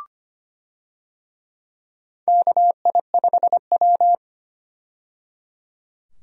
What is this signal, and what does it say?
Morse code, heard as a keyed tone, about 25 wpm, 700 Hz. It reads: KI5W
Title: USA Call Signs - 4 Characters - 25wpm